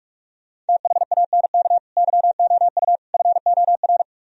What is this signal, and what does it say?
Morse code, heard as a keyed tone, about 58 wpm, 700 Hz. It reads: THANK YOU FOR